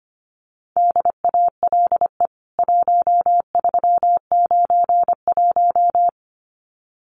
Morse code, heard as a keyed tone, about 25 wpm, 700 Hz. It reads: DALE 1391
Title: CWT Exchange 25wpm